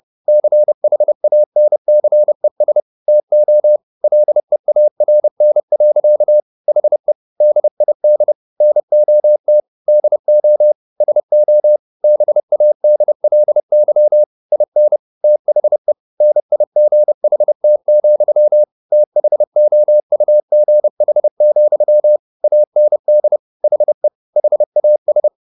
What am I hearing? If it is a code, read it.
Morse code, heard as a keyed tone, about 30 wpm, 600 Hz. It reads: CHANCES TO LEARN. HE DID NOT DO SO BADLY IN THE NIGHT, THOUGH, AND HE HAS